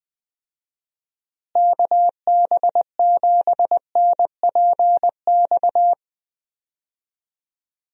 Morse code, heard as a keyed tone, about 21 wpm, 700 Hz. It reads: KB7NPX
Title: USA Call Signs - 6 Characters - Repeated 3x - 20wpm